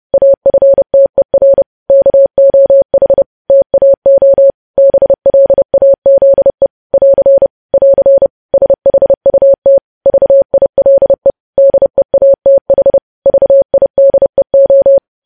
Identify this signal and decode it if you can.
Morse code, heard as a keyed tone, about 30 wpm, 550 Hz. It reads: AFTER KOH TAO BLAZE <AR> <AR> SHUT VILE DEATH VIDEO
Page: news headlines in morse code at 30 wpm